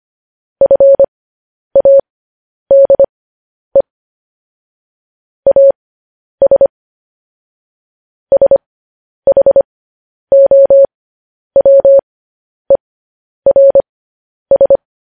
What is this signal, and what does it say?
Morse code, heard as a keyed tone, about 25 wpm, spaced as for 10 wpm, 550 Hz. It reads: FADE AS SHOWERS
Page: news headlines in morse code at 10 wpm